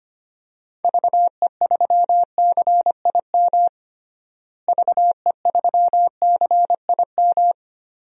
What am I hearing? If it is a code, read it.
Morse code, heard as a keyed tone, about 25 wpm, 700 Hz. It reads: VE3CIM VE3CIM
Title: DXCC-100 Call Signs - 6 Characters - Repeated 2x - 25wpm